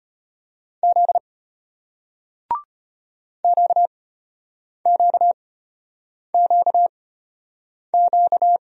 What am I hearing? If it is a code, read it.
Morse code, heard as a keyed tone, about 34 wpm, spaced as for 8 wpm, 700 Hz. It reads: Z QQQQ